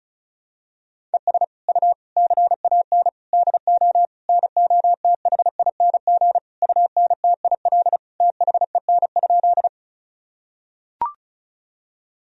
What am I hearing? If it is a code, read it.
Morse code, heard as a keyed tone, about 34 wpm, 700 Hz. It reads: ES U CAN DO NOTHING UNTIL THEN?
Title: Sentences from Top 500 Words and Abbreviations 35wpm